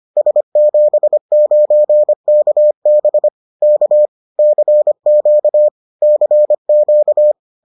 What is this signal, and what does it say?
Morse code, heard as a keyed tone, about 25 wpm, 600 Hz. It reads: S79KB K CQ CQ